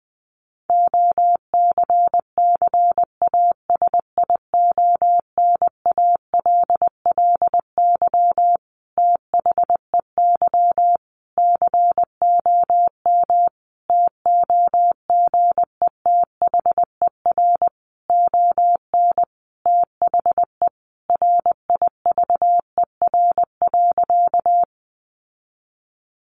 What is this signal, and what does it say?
Morse code, heard as a keyed tone, about 20 wpm, 700 Hz. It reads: OCCASIONALLY THEY COM TOGETHER ON THE RIVER.
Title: Sentences from Top 1300 Words 20wpm